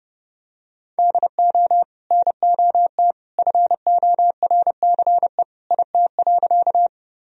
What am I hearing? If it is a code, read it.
Morse code, heard as a keyed tone, about 30 wpm, 700 Hz. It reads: DO NOT FORCE IT.
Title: Sentences from Top 1000 Words - Review Part 3 - 30wpm